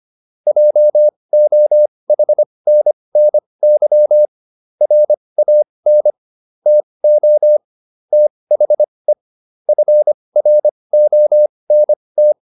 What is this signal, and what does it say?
Morse code, heard as a keyed tone, about 25 wpm, spaced as for 20 wpm, 600 Hz. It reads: JOHNNY RAN TO THE FRONT